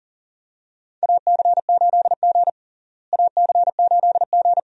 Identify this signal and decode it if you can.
Morse code, heard as a keyed tone, about 37 wpm, 700 Hz. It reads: AC8G AC8G
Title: USA Call Signs - 4 Characters - Repeated 2x - 40wpm